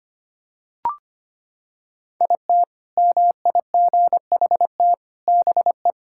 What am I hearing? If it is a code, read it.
Morse code, heard as a keyed tone, about 25 wpm, 700 Hz. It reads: IT MIGHT BE
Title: Sentences from Top 1000 Words - Review Part 6 - 25wpm